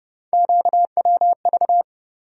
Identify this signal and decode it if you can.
Morse code, heard as a keyed tone, about 31 wpm, 700 Hz. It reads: QWV
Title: USA Call Sign Postfixes - 3 Letters 30wpm